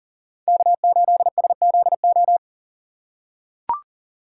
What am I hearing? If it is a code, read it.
Morse code, heard as a keyed tone, about 37 wpm, 700 Hz. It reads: K8SZO
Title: USA Call Signs - 5 Characters - 2x Character Spacing - 40wpm